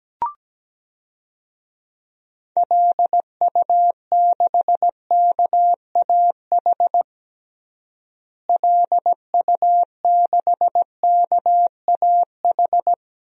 Morse code, heard as a keyed tone, about 17 wpm, 700 Hz. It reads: LU6KAH LU6KAH
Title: DXCC-100 Call Signs - 6 Characters - Repeated 2x - 17wpm